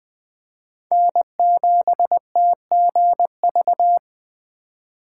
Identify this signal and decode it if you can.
Morse code, heard as a keyed tone, about 20 wpm, 700 Hz. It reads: N7TGV
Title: USA Call Signs - 5 Characters - 4x Character Spacing - 20wpm